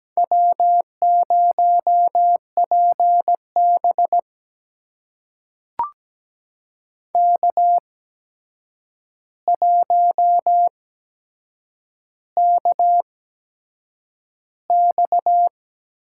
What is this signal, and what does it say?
Morse code, heard as a keyed tone, about 17 wpm, spaced as for 7 wpm, 700 Hz. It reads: W0PB K1KX
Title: USA Call Signs - 4 Characters - 8x Character Spacing - 17wpm